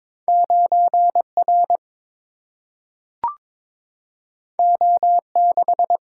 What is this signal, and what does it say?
Morse code, heard as a keyed tone, about 22 wpm, 700 Hz. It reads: 9R O6
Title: Single-Letter Single-Number 22wpm